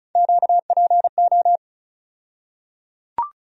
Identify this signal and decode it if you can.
Morse code, heard as a keyed tone, about 36 wpm, 700 Hz. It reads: QPO
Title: USA Call Sign Postfixes - 3 Letters 35wpm